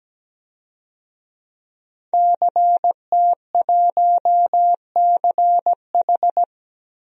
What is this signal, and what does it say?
Morse code, heard as a keyed tone, about 17 wpm, 700 Hz. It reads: CT1CH
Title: DXCC-100 Call Signs - 5 Characters - 17wpm